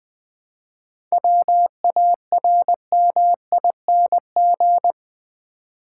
Morse code, heard as a keyed tone, about 20 wpm, 700 Hz. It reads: WARMING
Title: 7 Letter Words - First 3 Letters Given - 20wpm